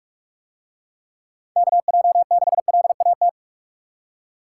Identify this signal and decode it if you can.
Morse code, heard as a keyed tone, about 44 wpm, 700 Hz. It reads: KJ6LAT